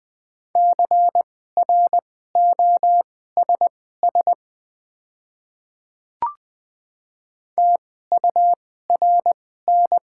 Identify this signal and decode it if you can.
Morse code, heard as a keyed tone, about 20 wpm, spaced as for 14 wpm, 700 Hz. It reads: CROSS TURN
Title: 2x Character Spacing - Top 300 Words - 20wpm